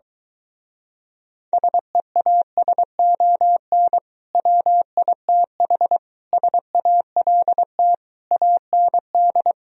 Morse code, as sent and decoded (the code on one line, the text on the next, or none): ... .- ... --- -. / .-- .. - .... / ... .- .-.. - / .- -. -..
SEASON WITH SALT AND